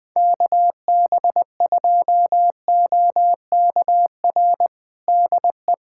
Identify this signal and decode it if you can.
Morse code, heard as a keyed tone, about 21 wpm, 700 Hz. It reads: KB2OKR DE
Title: USA Callsign DE Callsign - 6 Characters - 20wpm